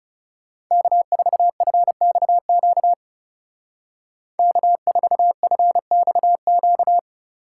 Morse code, heard as a keyed tone, about 39 wpm, 700 Hz. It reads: K4FXQ K4FXQ